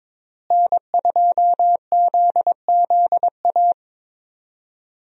Morse code, heard as a keyed tone, about 22 wpm, 700 Hz. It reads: N2ZZA
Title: USA Call Signs - 5 Characters - Repeated 2x - 22wpm